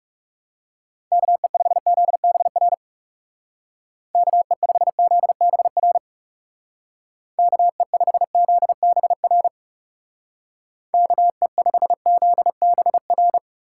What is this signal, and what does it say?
Morse code, heard as a keyed tone, about 38 wpm, 700 Hz. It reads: KE5ZBR KE5ZBR KE5ZBR KE5ZBR